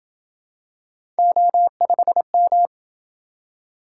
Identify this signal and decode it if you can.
Morse code, heard as a keyed tone, about 26 wpm, 700 Hz. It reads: O5M